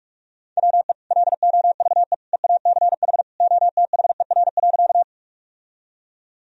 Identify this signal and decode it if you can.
Morse code, heard as a keyed tone, about 42 wpm, 700 Hz. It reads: WE LOVE EACH OTHER.
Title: Sentences from Top 1000 Words - Review Part 1 - 45wpm